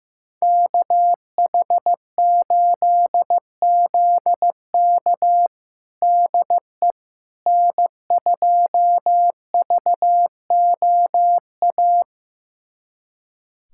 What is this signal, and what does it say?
Morse code, heard as a keyed tone, about 15 wpm, 700 Hz. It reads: KH8ZK DE N2VOA
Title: USA Callsign DE Callsign - 5 Characters - 15wpm